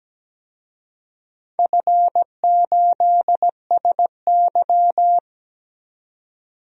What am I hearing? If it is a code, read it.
Morse code, heard as a keyed tone, about 17 wpm, 700 Hz. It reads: F8SY
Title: DXCC-100 Call Signs - 4 Characters - 17wpm